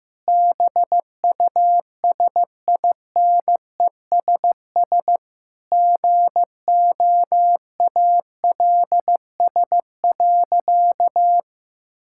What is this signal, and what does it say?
Morse code, heard as a keyed tone, about 16 wpm, 700 Hz. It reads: BUSINESS GOALS.
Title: Sentences from Top 1000 Words - Review Part 6 - 15wpm